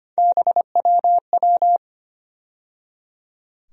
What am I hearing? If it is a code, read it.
Morse code, heard as a keyed tone, about 25 wpm, 700 Hz. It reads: BWW
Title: USA Call Sign Postfixes - 3 Letters 25wpm